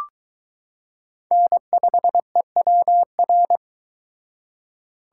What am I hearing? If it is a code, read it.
Morse code, heard as a keyed tone, about 22 wpm, 700 Hz. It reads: N5EWR